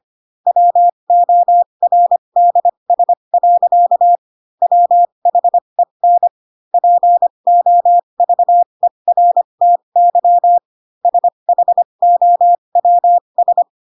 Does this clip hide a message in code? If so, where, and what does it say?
Yes: Morse code from 0.5 to 13.6 s, heard as a keyed tone, about 25 wpm, spaced as for 22 wpm, 700 Hz. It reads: WORDS. WHEN POVERTY SHOWS